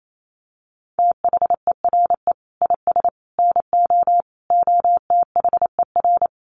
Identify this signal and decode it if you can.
Morse code, heard as a keyed tone, about 28 wpm, 700 Hz. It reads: THERE IS NO OTHER
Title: Sentences from Top 1300 Words 28wpm